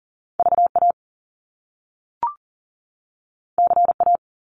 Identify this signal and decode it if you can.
Morse code, heard as a keyed tone, about 38 wpm, 700 Hz. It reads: VA CA
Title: US States 40wpm